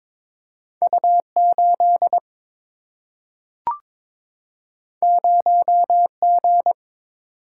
Morse code, heard as a keyed tone, about 22 wpm, 700 Hz. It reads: U 8 0 G
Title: Single-Letter Single-Number 22wpm